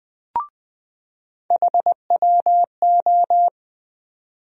Morse code, heard as a keyed tone, about 20 wpm, 700 Hz. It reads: HWO
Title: USA Call Sign Postfixes - 3 Letters 20wpm